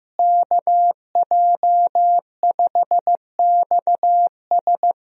Morse code, heard as a keyed tone, about 15 wpm, 700 Hz. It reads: KJ5XS
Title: USA Call Signs - 5 Characters - Repeated 2x - 15wpm